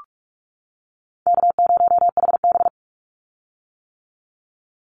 Morse code, heard as a keyed tone, about 44 wpm, 700 Hz. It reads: K0HB
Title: Call Signs - 45wpm